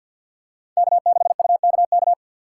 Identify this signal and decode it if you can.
Morse code, heard as a keyed tone, about 50 wpm, 700 Hz. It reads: K6UKK
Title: USA Call Signs - 5 Characters - 3x Character Spacing - 50wpm